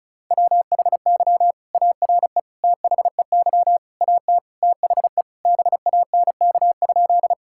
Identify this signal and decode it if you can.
Morse code, heard as a keyed tone, about 36 wpm, 700 Hz. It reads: WHY ARE THEY AT THE BANK?
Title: Sentences from Top 1000 Words - Review Part 6 - 35wpm